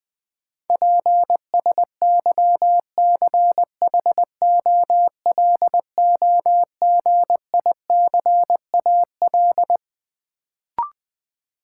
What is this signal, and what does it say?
Morse code, heard as a keyed tone, about 20 wpm, 700 Hz. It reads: PSYCHOLOGICAL
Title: Sentences from Top 1400 Words - New Word Drill Alt - 20wpm